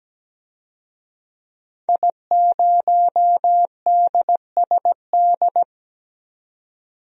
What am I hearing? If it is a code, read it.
Morse code, heard as a keyed tone, about 17 wpm, 700 Hz. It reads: I0DSD
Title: DXCC-100 Call Signs - 5 Characters - 17wpm